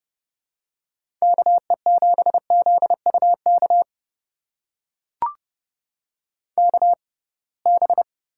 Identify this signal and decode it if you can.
Morse code, heard as a keyed tone, about 30 wpm, spaced as for 23 wpm, 700 Hz. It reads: KE7ZUK KB
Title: USA Call Signs - 6 Characters - 6x Character Spacing - 30wpm